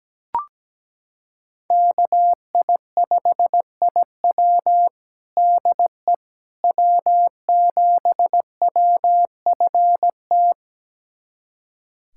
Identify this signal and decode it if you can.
Morse code, heard as a keyed tone, about 17 wpm, 700 Hz. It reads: KI5IW DE W7WFT
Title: USA Callsign DE Callsign 17wpm